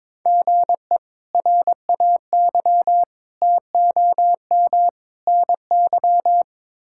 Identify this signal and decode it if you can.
Morse code, heard as a keyed tone, about 22 wpm, 700 Hz. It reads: GE RAY TOM NY
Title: K1USN SST QSO Practice - 22wpm